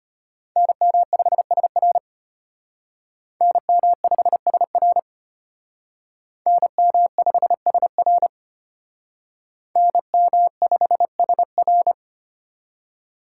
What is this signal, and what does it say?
Morse code, heard as a keyed tone, about 33 wpm, 700 Hz. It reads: NM5SR NM5SR NM5SR NM5SR